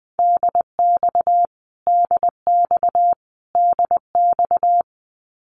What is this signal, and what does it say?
Morse code, heard as a keyed tone, about 20 wpm, 700 Hz. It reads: DX DX DX
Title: Abbreviations Repeated 3x - 20wpm